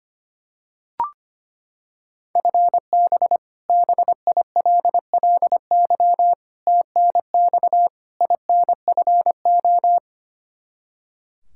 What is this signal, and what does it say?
Morse code, heard as a keyed tone, about 25 wpm, 700 Hz. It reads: FB BILLY TNX INFO
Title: K1USN SST QSO Practice 25wpm